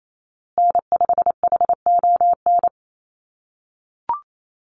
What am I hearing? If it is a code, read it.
Morse code, heard as a keyed tone, about 28 wpm, 700 Hz. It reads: N5HON
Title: USA Call Signs Repeated 2x 28wpm